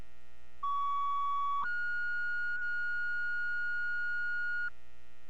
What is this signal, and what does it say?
Two-tone page, heard as a one-second tone, then a three-second tone, about 1130 then 1440 Hz, with noise. It is heard from 0.6 to 4.7 s.